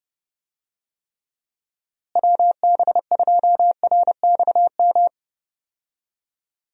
Morse code, heard as a keyed tone, about 30 wpm, 700 Hz. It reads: WB2RXM
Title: USA Call Signs - 6 Characters - Repeated 2x - 30wpm